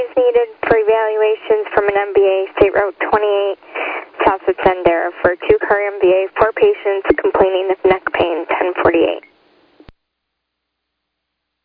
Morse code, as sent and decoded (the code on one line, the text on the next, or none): none